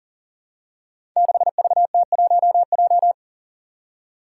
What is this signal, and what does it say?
Morse code, heard as a keyed tone, about 40 wpm, 700 Hz. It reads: 6VT1J
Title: Five Challenging Letter-Number Combinations 40wpm